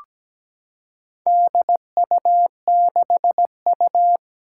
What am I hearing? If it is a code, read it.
Morse code, heard as a keyed tone, about 17 wpm, 700 Hz. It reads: DU6U
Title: Four Challenging Letter-Number Combinations 17wpm